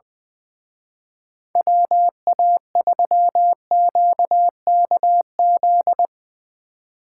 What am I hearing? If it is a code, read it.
Morse code, heard as a keyed tone, about 22 wpm, 700 Hz. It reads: WA3QKZ
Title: Speed Racing - USA 2x3 Call Signs 20wpm